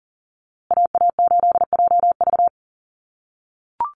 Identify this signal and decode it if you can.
Morse code, heard as a keyed tone, about 40 wpm, 700 Hz. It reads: AA8JV
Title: USA Call Signs Repeated 2x 40wpm